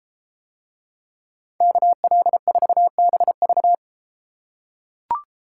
Morse code, heard as a keyed tone, about 32 wpm, 700 Hz. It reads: KL4BV